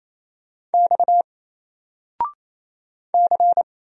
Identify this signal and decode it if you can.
Morse code, heard as a keyed tone, about 27 wpm, 700 Hz. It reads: X C
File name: Single-Letter-Number-28wpm